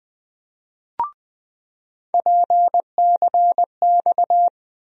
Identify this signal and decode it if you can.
Morse code, heard as a keyed tone, about 20 wpm, 700 Hz. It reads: PCX
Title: USA Call Sign Postfixes - 3 Letters 20wpm